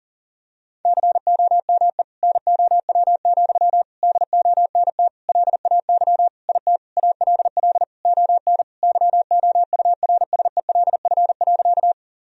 Morse code, heard as a keyed tone, about 40 wpm, 700 Hz. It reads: COME NOW, DONT LAY IT ALL ON YOURSELF.
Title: Sentences from Top 1000 Words - Review Part 2 - 40wpm